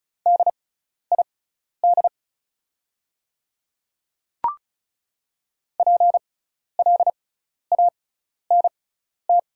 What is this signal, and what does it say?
Morse code, heard as a keyed tone, about 36 wpm, spaced as for 12 wpm, 690 Hz. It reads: DID PLANT
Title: 6x Character Spacing - Top 300 Words - 35wpm